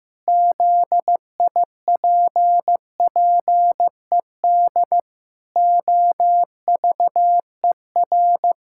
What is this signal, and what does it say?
Morse code, heard as a keyed tone, about 15 wpm, 700 Hz. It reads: ZIPPED OVER
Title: Pangram Challenge v2 - 15wpm